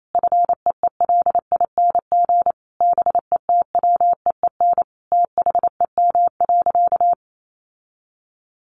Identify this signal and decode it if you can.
Morse code, heard as a keyed tone, about 28 wpm, 700 Hz. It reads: FEELING BETWEEN THEM.
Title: Sentences from Top 1400 Words 28wpm